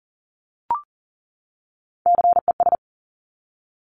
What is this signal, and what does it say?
Morse code, heard as a keyed tone, about 40 wpm, 700 Hz. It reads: CES
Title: Top Three Letter Combinations in Words 40wpm